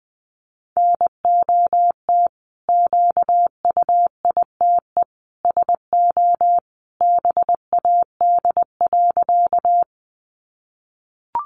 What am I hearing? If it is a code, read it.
Morse code, heard as a keyed tone, about 20 wpm, 700 Hz. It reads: NOT QUITE SO BAD.
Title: Sentences from Top 700 Words 20wpm